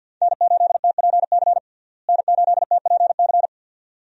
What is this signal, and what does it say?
Morse code, heard as a keyed tone, about 52 wpm, 700 Hz. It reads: N8TPC N8TPC
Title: USA Call Signs - 5 Characters - Repeated 2x - 50wpm